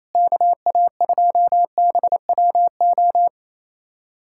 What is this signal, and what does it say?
Morse code, heard as a keyed tone, about 28 wpm, 700 Hz. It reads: KA2BWO
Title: USA Callsign DE Callsign - 6 Characters - 28wpm